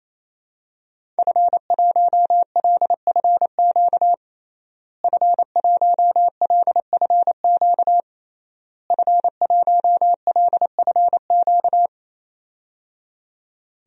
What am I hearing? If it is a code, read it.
Morse code, heard as a keyed tone, about 27 wpm, 700 Hz. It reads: F1LFQ F1LFQ F1LFQ
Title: DXCC-100 Call Signs - 5 Characters - Repeated 3x - 28wpm